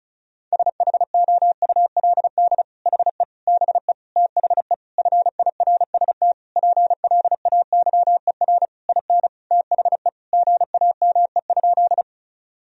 Morse code, heard as a keyed tone, about 34 wpm, 700 Hz. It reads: SHOULD HE BE THE FIRST PLAYER IN THE GAME?
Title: Sentences from Top 1000 Words - Review Part 4 - 35wpm